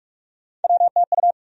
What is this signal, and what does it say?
Morse code, heard as a keyed tone, about 46 wpm, 700 Hz. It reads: WTU
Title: USA Call Sign Postfixes - 3 Letters 45wpm